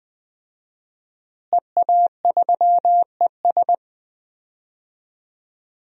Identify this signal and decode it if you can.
Morse code, heard as a keyed tone, about 20 wpm, 700 Hz. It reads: EA3ES
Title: DXCC-100 Call Signs - 5 Characters - Repeated 2x - 20wpm